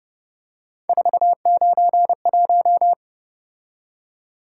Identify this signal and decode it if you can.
Morse code, heard as a keyed tone, about 29 wpm, 700 Hz. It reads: 491